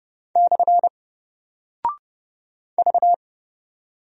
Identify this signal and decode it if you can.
Morse code, heard as a keyed tone, about 30 wpm, 700 Hz. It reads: / V
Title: Single Character (Letter:Number:Punctuation:Prosign) Alt 30wpm